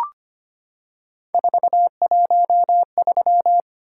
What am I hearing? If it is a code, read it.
Morse code, heard as a keyed tone, about 25 wpm, 700 Hz. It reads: 413
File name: Three Letter-Number Combinations - 25wpm